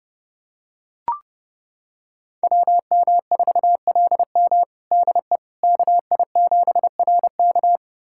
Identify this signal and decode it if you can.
Morse code, heard as a keyed tone, about 30 wpm, 700 Hz. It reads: WM4LM DE KI7RK
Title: USA Callsign DE Callsign - 5 Characters - 30wpm